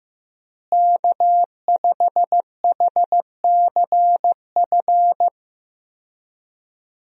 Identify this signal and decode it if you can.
Morse code, heard as a keyed tone, about 15 wpm, 700 Hz. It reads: K5HCF
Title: USA Call Signs - 5 Characters - Repeated 3x - 15wpm